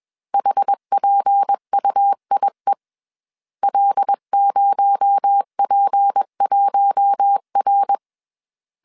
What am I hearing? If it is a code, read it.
Morse code, heard as a keyed tone, about 21 wpm, 800 Hz. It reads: HPUIE L0P1R
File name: CW / DET / 1 6 1500C S B 20wpm 2048